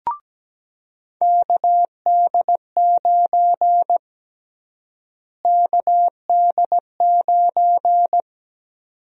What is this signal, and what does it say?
Morse code, heard as a keyed tone, about 17 wpm, 700 Hz. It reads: KD9 KD9
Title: USA Call Sign Prefixes - Repeated 2x - 17wpm